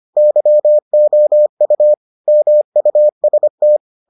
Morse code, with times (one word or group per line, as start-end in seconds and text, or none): 0.2-1.9 YOU
2.3-3.8 MUST